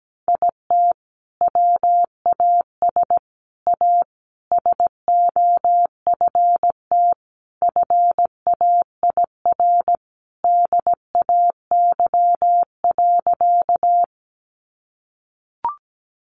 Morse code, heard as a keyed tone, about 17 wpm, 700 Hz. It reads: IT WAS A SOFT FAIR DAY.
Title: Sentences from Top 1500 Words 17wpm